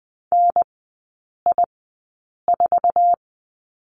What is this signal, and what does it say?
Morse code, heard as a keyed tone, about 20 wpm, spaced as for 8 wpm, 700 Hz. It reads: NI4
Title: 178 Lesson 29e - Slash - Three Characters 20wpm